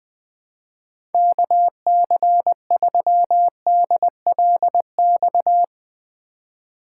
Morse code, heard as a keyed tone, about 21 wpm, 700 Hz. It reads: KC3DLX
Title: USA Call Signs - 6 Characters - 4x Character Spacing - 20wpm